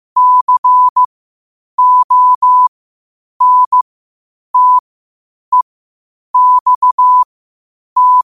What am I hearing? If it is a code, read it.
Morse code, heard as a keyed tone, about 15 wpm, spaced as for 8 wpm, 1000 Hz. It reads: CONTEXT
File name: qotdfiles / 8wpm